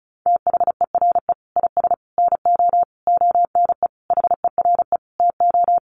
Morse code, heard as a keyed tone, about 34 wpm, 700 Hz. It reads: THERE IS NO ONE HERE TO
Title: Sentences from Top 200 Words 35wpm